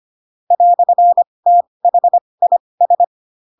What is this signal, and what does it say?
Morse code, heard as a keyed tone, about 25 wpm, spaced as for 20 wpm, 700 Hz. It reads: "THIS
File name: Pinocchio / Pinocchio ch1 15 25 ik1hge